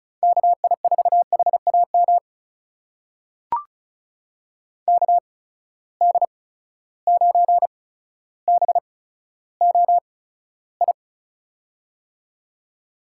Morse code, heard as a keyed tone, about 36 wpm, spaced as for 15 wpm, 700 Hz. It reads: KI4HAM KD9BOI
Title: USA Call Signs - 6 Characters - 8x Character Spacing - 35wpm